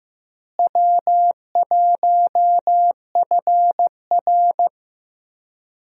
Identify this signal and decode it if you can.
Morse code, heard as a keyed tone, about 15 wpm, 700 Hz. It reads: W1FR
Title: USA Call Signs - 4 Characters - Repeated 3x - 15wpm